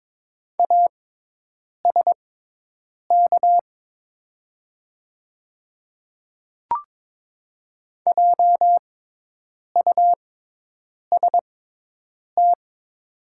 Morse code, heard as a keyed tone, about 22 wpm, spaced as for 8 wpm, 700 Hz. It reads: ASK JUST
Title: 6x Character Spacing - Top 200 Words - 22wpm